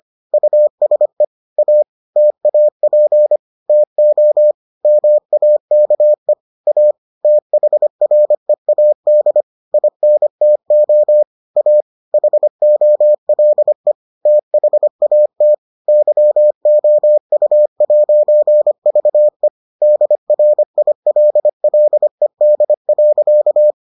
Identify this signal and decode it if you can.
Morse code, heard as a keyed tone, about 25 wpm, 600 Hz. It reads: USE A TAP TO MAKE A THREAD INTO A HOLE THAT YOU'VE DRILLED.